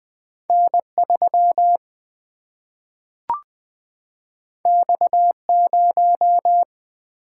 Morse code, heard as a keyed tone, about 20 wpm, 700 Hz. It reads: N 3 X 0
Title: Single-Letter Single-Number 20wpm